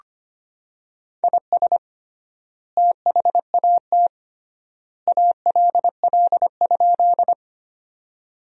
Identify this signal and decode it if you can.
Morse code, heard as a keyed tone, about 25 wpm, 700 Hz. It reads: IS THAT ALL?
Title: 3x Word-Spacing - Sentences from Top 100 Words - 25wpm